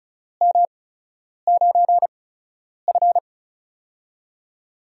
Morse code, heard as a keyed tone, about 34 wpm, spaced as for 9 wpm, 700 Hz. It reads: M9F
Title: USA Call Signs - 4 Characters - 8x Character Spacing - 35wpm